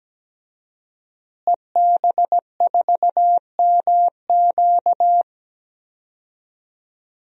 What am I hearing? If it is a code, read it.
Morse code, heard as a keyed tone, about 17 wpm, 700 Hz. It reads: EB4MQ
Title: DXCC-100 Call Signs - 5 Characters - Repeated 3x - 17wpm